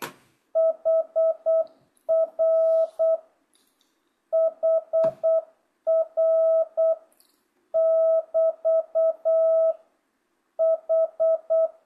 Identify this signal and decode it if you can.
Morse code, heard as a keyed tone, about 8 wpm, 660 Hz. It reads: HR HR <BT> H
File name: jh8cbh / 2025cw